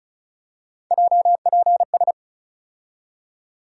Morse code, heard as a keyed tone, about 36 wpm, 700 Hz. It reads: JPS
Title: USA Call Sign Postfixes - 3 Letters 35wpm